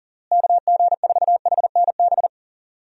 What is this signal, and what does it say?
Morse code, heard as a keyed tone, about 40 wpm, 700 Hz. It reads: KG4HNB
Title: USA Call Signs - 6 Characters - Repeated 3x - 40wpm